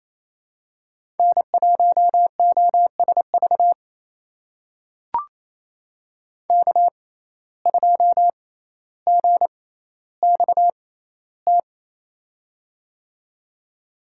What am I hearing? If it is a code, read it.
Morse code, heard as a keyed tone, about 28 wpm, spaced as for 14 wpm, 700 Hz. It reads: N1OSV K2GXT
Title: USA Call Signs - 5 Characters - 6x Character Spacing - 28wpm